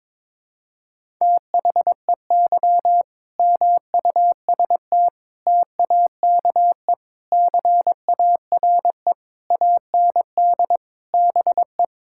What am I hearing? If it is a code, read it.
Morse code, heard as a keyed tone, about 22 wpm, 700 Hz. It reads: THEY MUST TAKE CARE AND BE
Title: Sentences from Top 1000 Words - Review Part 6 - 22wpm